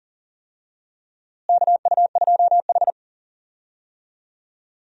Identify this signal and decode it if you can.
Morse code, heard as a keyed tone, about 40 wpm, 700 Hz. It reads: KU2H
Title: USA Call Signs - 4 Characters - Repeated 3x - 40wpm